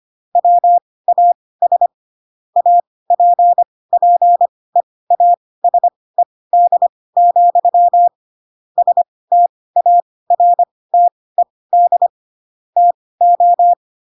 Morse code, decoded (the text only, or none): WAS APPEASED, STARTED TO